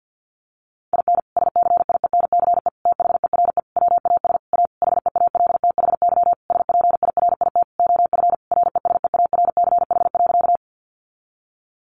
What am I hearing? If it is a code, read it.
Morse code, heard as a keyed tone, about 50 wpm, 700 Hz. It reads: IN SCIENCE THERE WAS A HEALTHY SPIRIT OF RESEARCH.
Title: Sentences from Top 1400 Words 50wpm